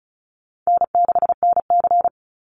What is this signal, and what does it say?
Morse code, heard as a keyed tone, about 34 wpm, 700 Hz. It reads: N6NC
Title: Call Signs Repeated 3x - 35wpm